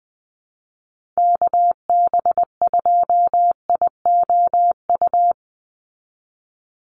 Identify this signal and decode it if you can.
Morse code, heard as a keyed tone, about 20 wpm, 700 Hz. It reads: KB2IOU
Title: USA Call Signs 20wpm